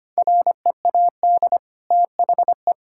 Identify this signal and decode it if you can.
Morse code, heard as a keyed tone, about 25 wpm, 700 Hz. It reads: READ THE
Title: Sentences from Top 1000 Words - Review Part 5 - 25wpm